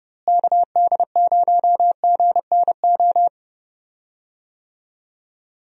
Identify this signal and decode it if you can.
Morse code, heard as a keyed tone, about 31 wpm, 700 Hz. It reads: KD0GNO
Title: USA Call Signs - 6 Characters - Repeated 2x - 30wpm